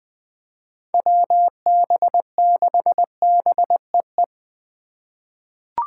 Morse code, heard as a keyed tone, about 20 wpm, 700 Hz. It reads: WB6BEE
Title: K1USN SST QSO Practice - 20wpm